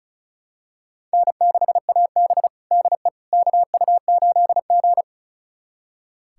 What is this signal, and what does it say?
Morse code, heard as a keyed tone, about 36 wpm, 700 Hz. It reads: N6AB DE KU8G
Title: USA Callsign DE Callsign - 4 Characters - 35wpm